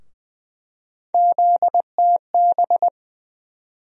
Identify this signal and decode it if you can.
Morse code, heard as a keyed tone, about 20 wpm, 700 Hz. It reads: ZTB
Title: USA Call Sign Postfixes - 3 Letters 20wpm